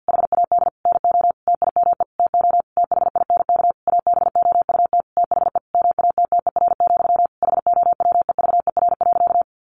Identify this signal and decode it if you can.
Morse code, heard as a keyed tone, about 50 wpm, 700 Hz. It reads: HAD NO TIME TO THINK ABOUT THE MATTER, HOWEVER.
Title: Sentences from Top 700 Words 50wpm